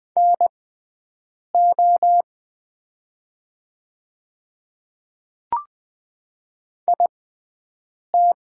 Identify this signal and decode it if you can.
Morse code, heard as a keyed tone, about 20 wpm, spaced as for 7 wpm, 700 Hz. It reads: NO IT
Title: 6x Character Spacing - Top 100 Words - 20wpm